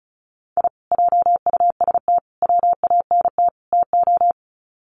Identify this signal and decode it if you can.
Morse code, heard as a keyed tone, about 34 wpm, 700 Hz. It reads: I JUST WANT TO
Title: Sets of 4 Words 35wpm